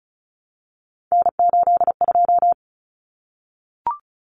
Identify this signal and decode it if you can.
Morse code, heard as a keyed tone, about 34 wpm, 700 Hz. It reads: N82
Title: Common Cut Numbers - 3-Digit 35wpm